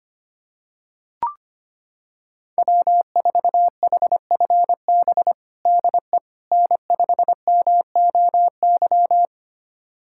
Morse code, heard as a keyed tone, about 25 wpm, 700 Hz. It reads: W4HFB DE N5MOY
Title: USA Callsign DE Callsign - 5 Characters - 25wpm